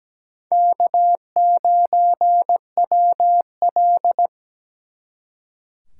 Morse code, heard as a keyed tone, about 20 wpm, 700 Hz. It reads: K9WL
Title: Speed Racing - USA Call Signs 17wpm